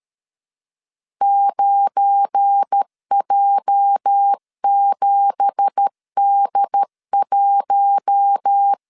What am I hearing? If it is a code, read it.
Morse code, heard as a keyed tone, about 13 wpm, 800 Hz. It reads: 9J7D1
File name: CW / DET / 1 9 1500C S B 12wpm 3819